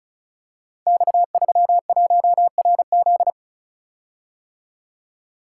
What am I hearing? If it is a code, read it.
Morse code, heard as a keyed tone, about 36 wpm, 700 Hz. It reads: X31RZ
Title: Code Groups - 5 Letter:Numbers - 35wpm